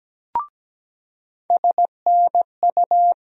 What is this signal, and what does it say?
Morse code, heard as a keyed tone, about 17 wpm, 700 Hz. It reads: SNU
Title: USA Call Sign Postfixes - 3 Letters 17wpm